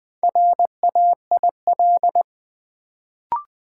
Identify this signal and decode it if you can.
Morse code, heard as a keyed tone, about 20 wpm, 700 Hz. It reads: RAIL
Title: Build A Word - 1 Letter At A Time - 4-Letter Words - 20wpm